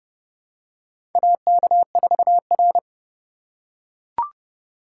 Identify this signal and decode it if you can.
Morse code, heard as a keyed tone, about 28 wpm, 700 Hz. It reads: AK4R
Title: USA Call Signs - 4 Characters - 3x Character Spacing - 30wpm